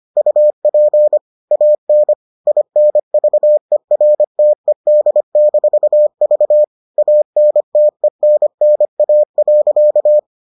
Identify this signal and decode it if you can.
Morse code, heard as a keyed tone, about 25 wpm, 600 Hz. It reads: UP AN INVERTED-V ANTENNA.